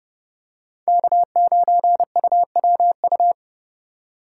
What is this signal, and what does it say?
Morse code, heard as a keyed tone, about 31 wpm, 700 Hz. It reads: K9UWU
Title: USA Call Signs - 5 Characters - 4x Character Spacing - 30wpm